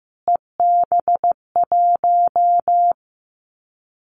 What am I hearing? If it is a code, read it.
Morse code, heard as a keyed tone, about 15 wpm, 700 Hz. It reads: EB1
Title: Callsign Prefixes for Top 25 DXCC - Repeated 3x - 15wpm